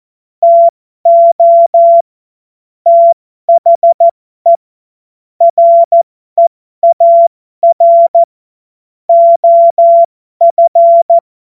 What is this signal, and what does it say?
Morse code, heard as a keyed tone, about 14 wpm, spaced as for 12 wpm, 680 Hz. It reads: TO THE REAR OF
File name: Morsekurs / Witze